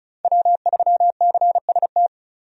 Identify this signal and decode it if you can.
Morse code, heard as a keyed tone, about 34 wpm, 700 Hz. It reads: W3CST
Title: USA Call Signs - 5 Characters - Repeated 2x - 35wpm